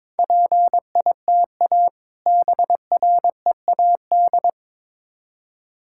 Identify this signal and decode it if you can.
Morse code, heard as a keyed tone, about 22 wpm, 700 Hz. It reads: PITA BREAD
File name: Difficult Words with Hints - 22wpm